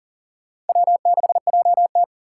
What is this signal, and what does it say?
Morse code, heard as a keyed tone, about 50 wpm, 700 Hz. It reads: W6JT
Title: Speed Racing - USA 1x2 Call Signs 40wpm